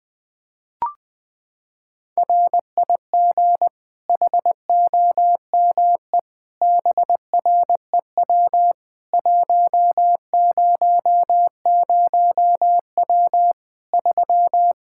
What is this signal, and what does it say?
Morse code, heard as a keyed tone, about 21 wpm, 700 Hz. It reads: RIG HOME BREW 100W 3
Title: Practice QSO - Exchange 2 - Info - 20wpm